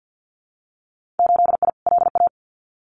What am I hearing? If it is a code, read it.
Morse code, heard as a keyed tone, about 50 wpm, 700 Hz. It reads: 8I LA
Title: Winter Field Day Exchanges Encore 50wpm